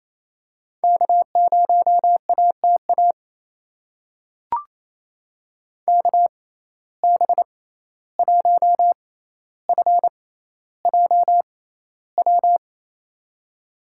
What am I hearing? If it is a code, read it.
Morse code, heard as a keyed tone, about 28 wpm, spaced as for 14 wpm, 700 Hz. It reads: K0ATA KB1FJW
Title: USA Call Signs - 6x Character Spacing - 28wpm